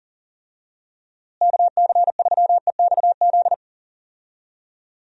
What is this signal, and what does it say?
Morse code, heard as a keyed tone, about 40 wpm, 700 Hz. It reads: KC3EXZ
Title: USA Call Signs - 6 Characters - Repeated 3x - 40wpm